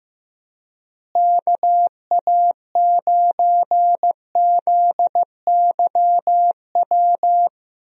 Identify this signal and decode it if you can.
Morse code, heard as a keyed tone, about 15 wpm, 700 Hz. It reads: KA9ZYW
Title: USA Call Signs - 6 Characters - Repeated 3x - 15wpm